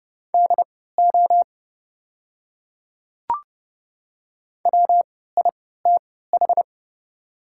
Morse code, heard as a keyed tone, about 30 wpm, spaced as for 17 wpm, 700 Hz. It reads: DO WITH
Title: 3x Character Spacing - Top 300 Words - 30wpm